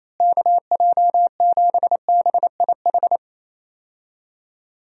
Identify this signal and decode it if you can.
Morse code, heard as a keyed tone, about 28 wpm, 700 Hz. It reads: KJ7BIH
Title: USA Call Signs - 6 Characters - Repeated 3x - 28wpm